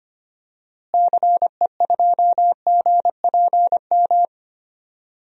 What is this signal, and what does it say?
Morse code, heard as a keyed tone, about 25 wpm, 700 Hz. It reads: CE2GPM
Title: DXCC-100 Call Signs - 6 Characters - Repeated 3x - 25wpm